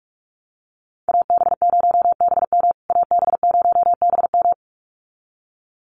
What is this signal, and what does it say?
Morse code, heard as a keyed tone, about 44 wpm, 700 Hz. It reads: AB0BM AB0BM
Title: USA Call Signs Repeated 2x 45wpm